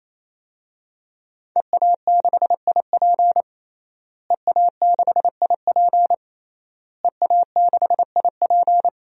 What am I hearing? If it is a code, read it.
Morse code, heard as a keyed tone, about 28 wpm, 700 Hz. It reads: EA6IP EA6IP EA6IP
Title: DXCC-100 Call Signs - 5 Characters - Repeated 3x - 28wpm